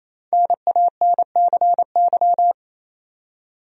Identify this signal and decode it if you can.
Morse code, heard as a keyed tone, about 28 wpm, 700 Hz. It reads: NANCY
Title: USA Common Names - Max 5 Letters - 28wpm